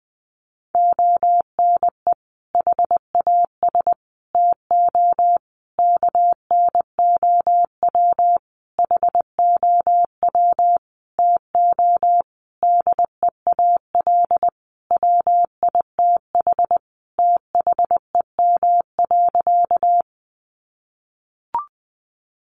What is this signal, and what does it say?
Morse code, heard as a keyed tone, about 20 wpm, 700 Hz. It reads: ONE HAS TO KNOW HOW TO DEAL WITH THEM.
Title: Sentences from Top 800 Words 20wpm